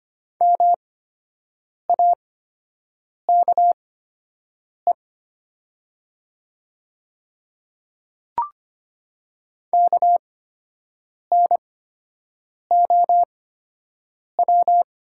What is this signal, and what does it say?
Morse code, heard as a keyed tone, about 25 wpm, spaced as for 7 wpm, 700 Hz. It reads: MAKE KNOW
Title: 8x Character Spacing - Top 100 Words - 25wpm